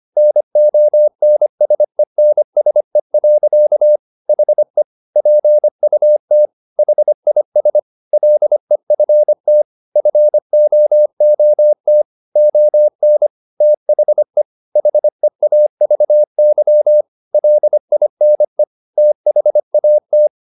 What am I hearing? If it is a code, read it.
Morse code, heard as a keyed tone, about 25 wpm, 600 Hz. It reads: NONSENSE. HE PUT HIS LEFT FOOT ON THE HEAVY LINE THAT
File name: the old man and the sea part 1 25wpm